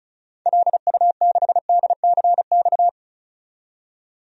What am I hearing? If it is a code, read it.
Morse code, heard as a keyed tone, about 34 wpm, 700 Hz. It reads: LU6DCX
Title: DXCC-100 Call Signs - 6 Characters - 35wpm